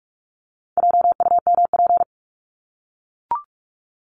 Morse code, heard as a keyed tone, about 44 wpm, 700 Hz. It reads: JUMP
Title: Top 1251-1500 Words 45wpm